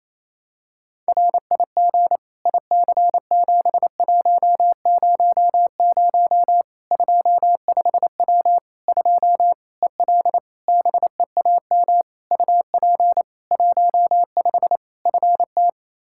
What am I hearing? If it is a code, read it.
Morse code, heard as a keyed tone, about 28 wpm, 700 Hz. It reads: RIG IC7100 25W 2 EL BEAM UP 15 FT
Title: K1USN SST QSO Practice 28wpm